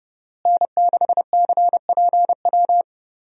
Morse code, heard as a keyed tone, about 30 wpm, 700 Hz. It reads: N6CPW
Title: USA Call Signs - 5 Characters - Repeated 2x - 30wpm